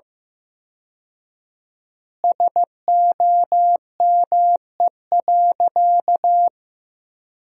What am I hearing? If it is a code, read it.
Morse code, heard as a keyed tone, about 16 wpm, 700 Hz. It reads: SOME.
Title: 4x Word-Spacing - Sentences from Top 100 Words - 15wpm